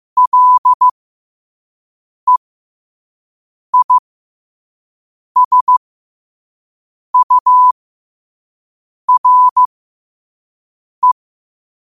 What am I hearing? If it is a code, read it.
Morse code, heard as a keyed tone, about 15 wpm, spaced as for 5 wpm, 1000 Hz. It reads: LEISURE
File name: qotdfiles / 5wpm